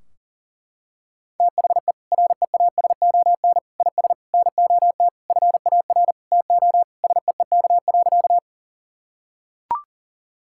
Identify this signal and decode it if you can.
Morse code, heard as a keyed tone, about 40 wpm, 700 Hz. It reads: THE REASON IS NOT FAR TO SEEK.
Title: Sentences from Top 1600 Words 40wpm